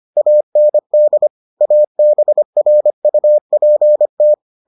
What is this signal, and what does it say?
Morse code, heard as a keyed tone, about 25 wpm, 600 Hz. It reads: AND ABRUPT